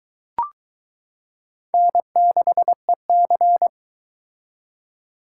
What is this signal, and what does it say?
Morse code, heard as a keyed tone, about 22 wpm, 700 Hz. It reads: N6EC